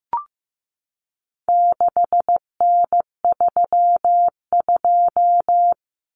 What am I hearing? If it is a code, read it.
Morse code, heard as a keyed tone, about 15 wpm, 700 Hz. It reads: 6N32
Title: Common Cut Numbers - 4-Digit 15wpm